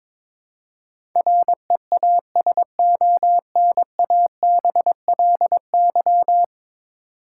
Morse code, heard as a keyed tone, about 22 wpm, 700 Hz. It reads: REASONABLY
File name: 10 Letter Words - First 3 Letters Given - 22wpm